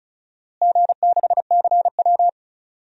Morse code, heard as a keyed tone, about 34 wpm, 700 Hz. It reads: G6CW
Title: DXCC-100 Call Signs - 4 Characters - Repeated 3x - 35wpm